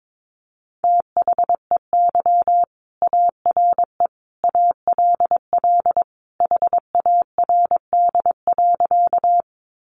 Morse code, heard as a keyed tone, about 22 wpm, 700 Hz. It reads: THEY ARE ALL HARD.
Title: Additional Easy Phrases 22wpm